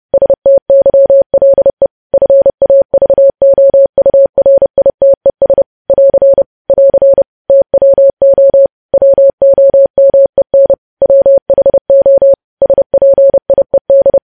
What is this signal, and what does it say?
Morse code, heard as a keyed tone, about 30 wpm, 550 Hz. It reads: STYLE FAVOURITES <AR> <AR> TWO WOMEN WHO SPIED
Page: news headlines in morse code at 30 wpm